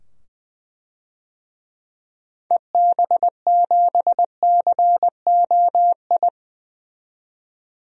Morse code, heard as a keyed tone, about 20 wpm, 700 Hz. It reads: EB7COI
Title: DXCC-100 Call Signs - 6 Characters - 20wpm